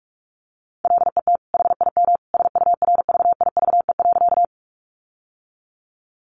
Morse code, heard as a keyed tone, about 44 wpm, 700 Hz. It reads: LET HIM SURVIVE.
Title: Sentences from Top 1300 Words 45wpm